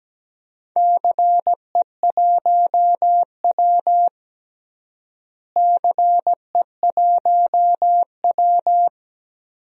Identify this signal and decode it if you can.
Morse code, heard as a keyed tone, about 17 wpm, 700 Hz. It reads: CE1W CE1W
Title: DXCC-100 Call Signs - 4 Characters - Repeated 2x - 17wpm